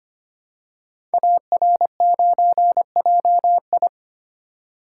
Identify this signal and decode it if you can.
Morse code, heard as a keyed tone, about 25 wpm, 700 Hz. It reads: AR9JI
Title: Code Groups - 5 Letter:Numbers - 25wpm